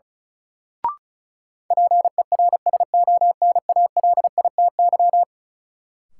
Morse code, heard as a keyed tone, about 34 wpm, 700 Hz. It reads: PERSONALITY
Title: Sentences from Top 1400 Words - New Word Drill Alt - 35wpm